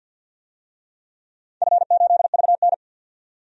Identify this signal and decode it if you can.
Morse code, heard as a keyed tone, about 52 wpm, 700 Hz. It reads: F8VN
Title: DXCC-100 Call Signs - 4 Characters - 50wpm